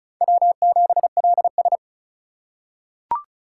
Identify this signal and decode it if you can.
Morse code, heard as a keyed tone, about 34 wpm, 700 Hz. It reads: W7LS
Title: USA Call Signs - 4 Characters - 2x Character Spacing - 35wpm